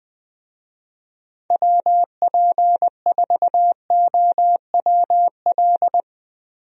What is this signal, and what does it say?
Morse code, heard as a keyed tone, about 20 wpm, 700 Hz. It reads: WP4OWL
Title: USA Call Signs - 6 Characters - Repeated 2x - 20wpm